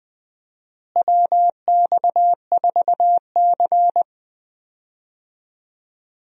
Morse code, heard as a keyed tone, about 20 wpm, 700 Hz. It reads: WX4C
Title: USA Call Signs - 4 Characters - Repeated 2x - 20wpm